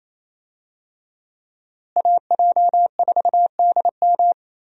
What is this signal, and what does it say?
Morse code, heard as a keyed tone, about 28 wpm, 700 Hz. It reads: AJ4DM
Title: USA Call Signs - 5 Characters - Repeated 2x - 28wpm